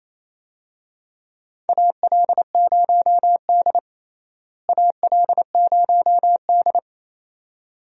Morse code, heard as a keyed tone, about 28 wpm, 700 Hz. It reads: AL0D AL0D
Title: USA Call Signs - 4 Characters - Repeated 2x - 28wpm